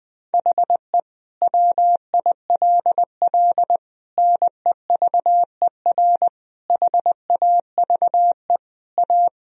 Morse code, heard as a keyed tone, about 20 wpm, 700 Hz. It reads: HE WILL NEVER HAVE A
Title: Sentences from Top 1000 Words - Review Part 2 - 20wpm